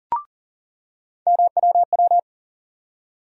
Morse code, heard as a keyed tone, about 40 wpm, 700 Hz. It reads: MWW
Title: USA Call Sign Postfixes - 3 Letters 40wpm